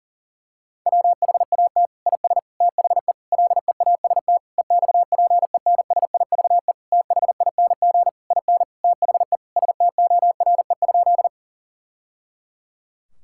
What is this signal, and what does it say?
Morse code, heard as a keyed tone, about 37 wpm, 700 Hz. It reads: WHAT IS THE LEAST EXPENSIVE THING IN THE STORE?
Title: Sentences from Top 1000 Words - Review Part 6 - 40wpm